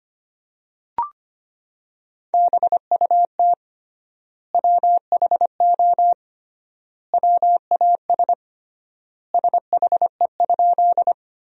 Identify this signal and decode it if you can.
Morse code, heard as a keyed tone, about 25 wpm, 700 Hz. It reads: BUT WHO WAS SHE?
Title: 3x Word-Spacing - Sentences from Top 100 Words - 25wpm